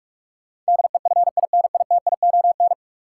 Morse code, heard as a keyed tone, about 41 wpm, 700 Hz. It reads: DEFINITION
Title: Sentences from Top 1000 Words - Review Part 4 - New Word Drill - 45wpm